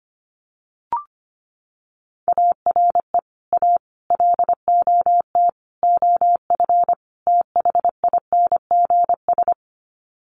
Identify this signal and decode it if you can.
Morse code, heard as a keyed tone, about 25 wpm, 700 Hz. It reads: ARE A LOT OF THINGS
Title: Sets of 5 Words Encore2 25wpm